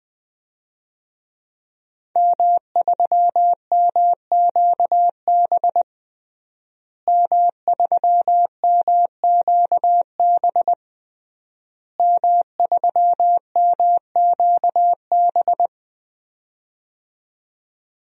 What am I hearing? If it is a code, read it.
Morse code, heard as a keyed tone, about 21 wpm, 700 Hz. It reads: M3MQB M3MQB M3MQB
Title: DXCC-100 Call Signs - 5 Characters - Repeated 3x - 20wpm